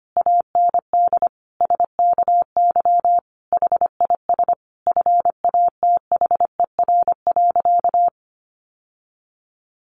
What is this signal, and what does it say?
Morse code, heard as a keyed tone, about 25 wpm, 700 Hz. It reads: AND SKY HIS FATHER.
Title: Sentences from Top 1200 Words 25wpm